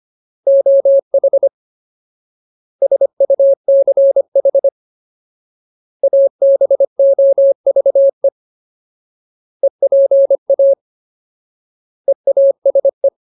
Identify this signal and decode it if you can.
Morse code, heard as a keyed tone, about 25 wpm, 550 Hz. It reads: OH SUCH ABOVE EPA EASE
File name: Ultimate Word List / 25WPM Ultimate